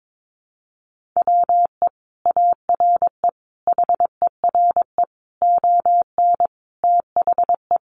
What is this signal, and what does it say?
Morse code, heard as a keyed tone, about 22 wpm, 700 Hz. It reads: WE ARE HERE ON THE
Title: Sentences from Top 1400 Words 22wpm